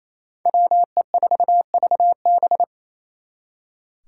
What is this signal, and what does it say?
Morse code, heard as a keyed tone, about 37 wpm, 700 Hz. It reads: WE4VB